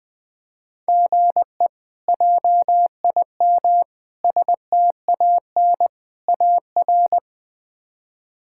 Morse code, heard as a keyed tone, about 21 wpm, 700 Hz. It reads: GE JIM STAN AR
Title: K1USN SST QSO Practice - 20wpm